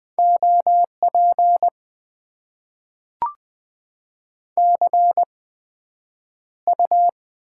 Morse code, heard as a keyed tone, about 20 wpm, spaced as for 9 wpm, 700 Hz. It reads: OP CU
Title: Top 90 QSO Elements - 8x Character Spacing - 20wpm